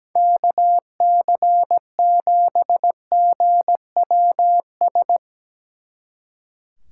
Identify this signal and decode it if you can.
Morse code, heard as a keyed tone, about 18 wpm, 700 Hz. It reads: KC7GWS
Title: Speed Racing - USA 2x3 Call Signs 17wpm